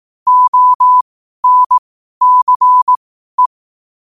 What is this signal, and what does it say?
Morse code, heard as a keyed tone, about 18 wpm, spaced as for 12 wpm, 1000 Hz. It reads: ONCE
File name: qotdfiles / 12wpm